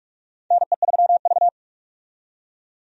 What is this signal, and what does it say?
Morse code, heard as a keyed tone, about 42 wpm, 700 Hz. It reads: NE3V
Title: USA Call Signs - 4 Characters - 45wpm